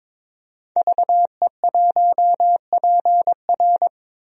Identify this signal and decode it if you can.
Morse code, heard as a keyed tone, about 22 wpm, 700 Hz. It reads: VE1PR
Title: DXCC-100 Call Signs - 5 Characters - Repeated 3x - 22wpm